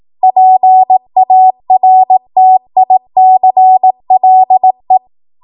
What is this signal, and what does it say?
Morse code, heard as a keyed tone, about 18 wpm, 750 Hz. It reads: PARTICLE
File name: Morse / 18WPM